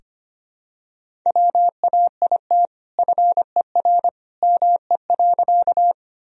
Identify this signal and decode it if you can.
Morse code, heard as a keyed tone, about 25 wpm, 700 Hz. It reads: WAIT FER ME.
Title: Sentences from Top 500 Words and Abbreviations 25wpm